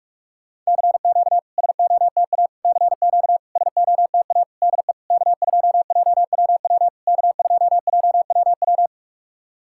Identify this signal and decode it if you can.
Morse code, heard as a keyed tone, about 46 wpm, 700 Hz. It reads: CQ SOTA CQ SOTA DE K2JWW K2JWW
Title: SOTA QSO Practice 45wpm